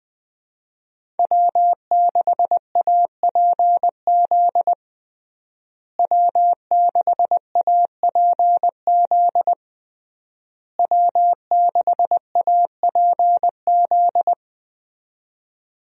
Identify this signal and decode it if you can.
Morse code, heard as a keyed tone, about 21 wpm, 700 Hz. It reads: W6APZ W6APZ W6APZ
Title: USA Call Signs - 5 Characters - Repeated 3x - 20wpm